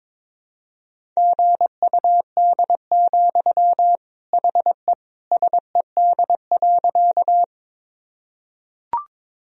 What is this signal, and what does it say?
Morse code, heard as a keyed tone, about 22 wpm, 700 Hz. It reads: GUD, HE SED.
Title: Sentences from Top 200 Words and Abbreviations 22wpm